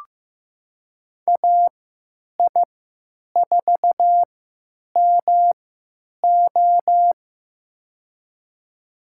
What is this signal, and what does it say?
Morse code, heard as a keyed tone, about 15 wpm, spaced as for 8 wpm, 700 Hz. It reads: AI4MO
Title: USA Call Signs - 5 Characters - 3x Character Spacing - 15wpm